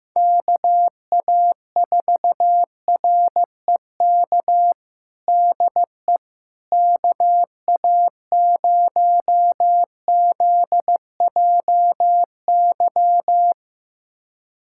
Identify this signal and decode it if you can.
Morse code, heard as a keyed tone, about 15 wpm, 700 Hz. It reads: KA4REK DE KA0ZJY
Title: USA Callsign DE Callsign 15wpm